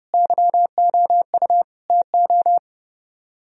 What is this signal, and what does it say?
Morse code, heard as a keyed tone, about 39 wpm, 700 Hz. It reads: YOU TO